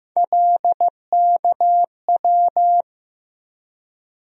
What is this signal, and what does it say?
Morse code, heard as a keyed tone, about 15 wpm, 700 Hz. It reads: LKW
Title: USA Call Sign Postfixes - 3 Letters 15wpm